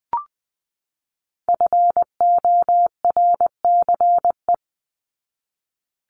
Morse code, heard as a keyed tone, about 20 wpm, 700 Hz. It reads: FORCE
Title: ED Suffix - Top 100 Words with Base Words 20wpm